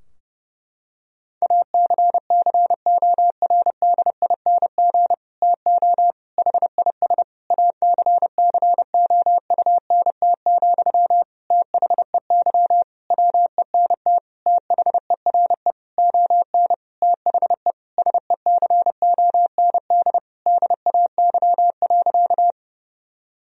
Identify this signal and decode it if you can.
Morse code, heard as a keyed tone, about 32 wpm, 700 Hz. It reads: ACCORDING TO HIS ACCOUNT, THEY WENT THERE ON THE SECOND DAY.
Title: Sentences from Top 1000 Words - Review Part 4 - 30wpm